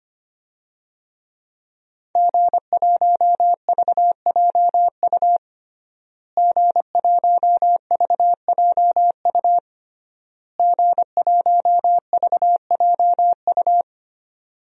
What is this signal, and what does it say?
Morse code, heard as a keyed tone, about 25 wpm, 700 Hz. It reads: G1VJU G1VJU G1VJU
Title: DXCC-100 Call Signs - 5 Characters - Repeated 3x - 25wpm